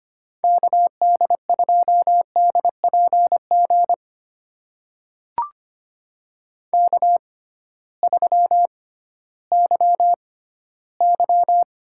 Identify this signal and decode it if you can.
Morse code, heard as a keyed tone, about 25 wpm, spaced as for 15 wpm, 700 Hz. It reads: KD2DPG K3YY
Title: USA Call Signs - 6x Character Spacing - 25wpm